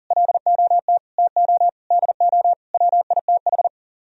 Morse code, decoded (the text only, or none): LOT TO DO WITH